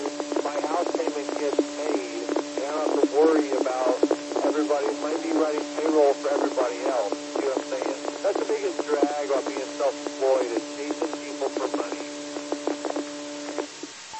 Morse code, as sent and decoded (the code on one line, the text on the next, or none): none